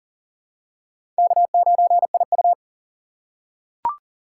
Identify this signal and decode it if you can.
Morse code, heard as a keyed tone, about 40 wpm, 700 Hz. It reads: K9IU
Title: USA Call Signs - 4 Characters - 3x Character Spacing - 40wpm